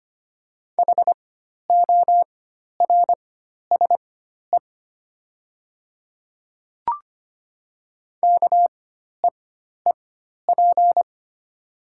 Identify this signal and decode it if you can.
Morse code, heard as a keyed tone, about 25 wpm, spaced as for 12 wpm, 690 Hz. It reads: HORSE KEEP
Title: 4x Character Spacing - Top 300 Words - 25wpm